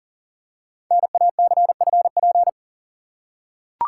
Morse code, heard as a keyed tone, about 40 wpm, 690 Hz. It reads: NACFP
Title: Code Groups - 5 Letter:Numbers - 40wpm